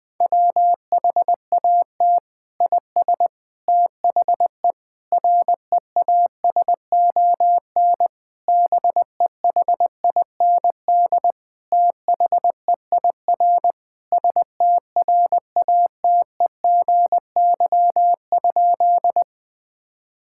Morse code, heard as a keyed tone, about 21 wpm, 700 Hz. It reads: WHAT IS THE REASON BEHIND THEIR STRATEGY?
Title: Sentences from Top 1000 Words - Review Part 5 - 20wpm